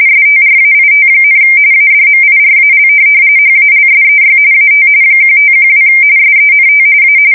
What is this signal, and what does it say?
RTTY, about 45.45 baud. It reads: STRAIT, ADMIRAL HALSEY AT 1512 SENT A
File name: fsk170 / USS-New-Jersey